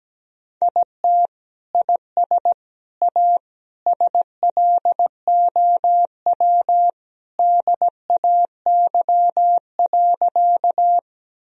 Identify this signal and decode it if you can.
Morse code, heard as a keyed tone, about 17 wpm, 700 Hz. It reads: IT IS A SLOW DAY.
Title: Sentences from Top 1000 Words - Review Part 2 - 17wpm